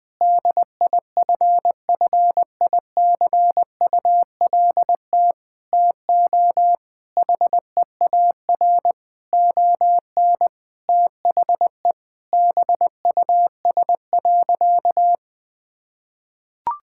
Morse code, heard as a keyed tone, about 20 wpm, 700 Hz. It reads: DIFFICULT TO HEAR ON THE BUS.
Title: Sentences from Top 1000 Words - Review Part 6 - 20wpm